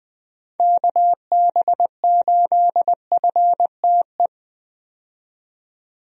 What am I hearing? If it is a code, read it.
Morse code, heard as a keyed tone, about 21 wpm, 700 Hz. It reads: KB8FTE
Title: USA Call Signs - 6 Characters - 20wpm